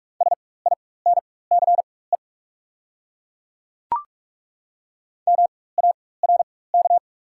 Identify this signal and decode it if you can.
Morse code, heard as a keyed tone, about 43 wpm, spaced as for 21 wpm, 690 Hz. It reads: SINCE MARK
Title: 4x Character Spacing - Top 300 Words - 45wpm